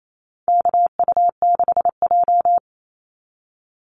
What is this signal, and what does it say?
Morse code, heard as a keyed tone, about 28 wpm, 700 Hz. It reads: KU6J
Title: NA SOTA Activator Callsigns 28wpm